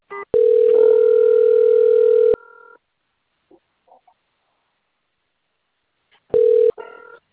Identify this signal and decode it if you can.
Ringback tone (North America), 440 + 480 Hz.